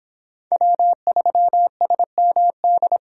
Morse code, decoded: W3SMD

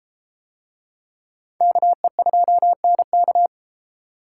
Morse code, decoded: KE2NK

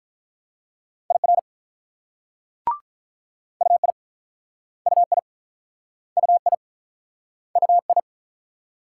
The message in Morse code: IR UI UI UI UI

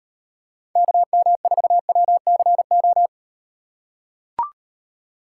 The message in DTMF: none

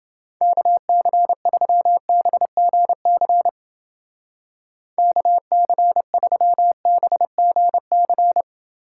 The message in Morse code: KC3BGC KC3BGC

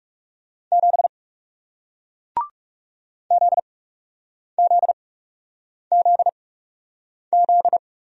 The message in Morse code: 7 ZZZZ

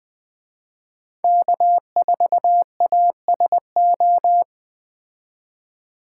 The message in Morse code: K4ASO

